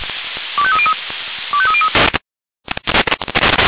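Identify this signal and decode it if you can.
Five-tone selective call (ZVEI), heard as quick runs of five tones, tones of about 70 ms. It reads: 25303 25303
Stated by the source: Five-tone ZVEI selective call received on SDR.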